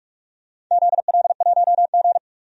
Morse code, heard as a keyed tone, about 46 wpm, 700 Hz. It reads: ZL1G
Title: DXCC-100 Call Signs - 4 Characters - 45wpm